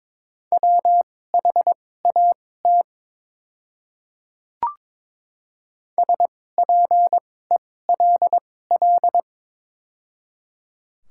Morse code, heard as a keyed tone, about 22 wpm, spaced as for 16 wpm, 700 Hz. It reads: WHAT SPELL
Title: 2x Character Spacing - Top 200 Words - 22wpm